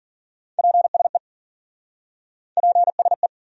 Morse code, heard as a keyed tone, about 52 wpm, 700 Hz. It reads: PSE PSE